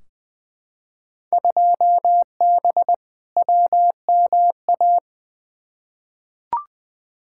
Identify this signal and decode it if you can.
Morse code, heard as a keyed tone, about 20 wpm, 700 Hz. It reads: 2B WMA
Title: Field Day Exchanges Encore - 20wpm